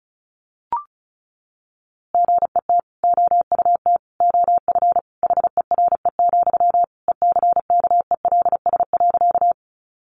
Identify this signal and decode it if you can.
Morse code, heard as a keyed tone, about 34 wpm, 700 Hz. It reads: GET OUT OF HERE, ECKELS.
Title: A Sound Of Thunder 35wpm